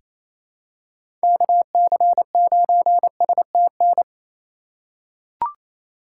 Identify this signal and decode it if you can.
Morse code, heard as a keyed tone, about 28 wpm, 700 Hz. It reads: KC9STN
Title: USA Call Signs - 6 Characters - 2x Character Spacing - 28wpm